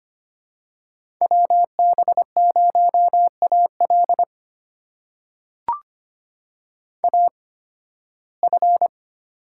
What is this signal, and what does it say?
Morse code, heard as a keyed tone, about 25 wpm, spaced as for 16 wpm, 700 Hz. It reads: WB0AL AF
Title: USA Call Signs - 5 Characters - 8x Character Spacing - 25wpm